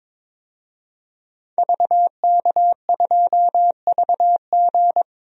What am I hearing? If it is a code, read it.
Morse code, heard as a keyed tone, about 21 wpm, 700 Hz. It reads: VK2VG
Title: DXCC-100 Call Signs - 5 Characters - 22wpm